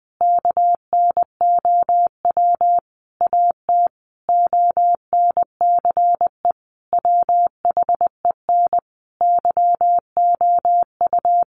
Morse code, heard as a keyed tone, about 20 wpm, 700 Hz. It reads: KNOW AT ONCE WHEN YOU
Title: Sentences from Top 300 Words 20wpm